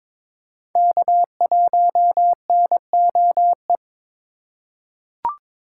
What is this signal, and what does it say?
Morse code, heard as a keyed tone, about 22 wpm, 700 Hz. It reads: K1NOE
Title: USA Call Signs - 5 Characters - 3x Character Spacing - 22wpm